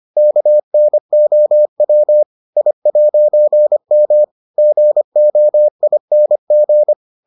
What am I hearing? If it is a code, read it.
Morse code, heard as a keyed tone, about 25 wpm, 600 Hz. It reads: KNOW I'M GOING